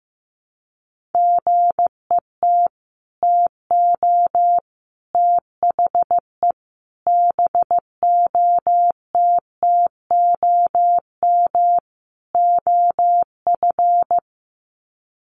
Morse code, heard as a keyed tone, about 15 wpm, 700 Hz. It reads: GET TO THE BOTTOM OF
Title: Sets of 5 Words Encore 15wpm